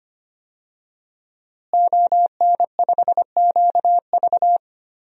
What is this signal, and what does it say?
Morse code, heard as a keyed tone, about 25 wpm, 700 Hz. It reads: ON5QV
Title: DXCC-100 Call Signs - 5 Characters - Repeated 2x - 25wpm